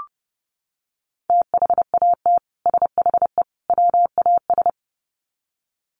Morse code, heard as a keyed tone, about 30 wpm, 700 Hz. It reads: THAT SHE WAS
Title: Sets of 3 Words 30wpm